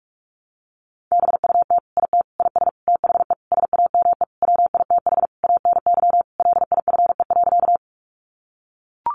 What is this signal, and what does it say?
Morse code, heard as a keyed tone, about 44 wpm, 700 Hz. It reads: BUT IT IS THE SAME WITH ANY LIFE.
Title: Sentences from Top 300 Words 45wpm